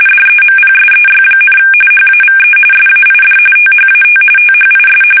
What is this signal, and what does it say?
RTTY, about 45.45 baud. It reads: DEMPSEY, HAVE SAID THAT THIS
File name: fsk850 / usnavy5